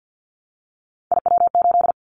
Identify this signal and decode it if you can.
Morse code, heard as a keyed tone, about 50 wpm, 700 Hz. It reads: IW8
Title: Callsign Prefixes for Top 25 DXCC - Repeated 3x - 50wpm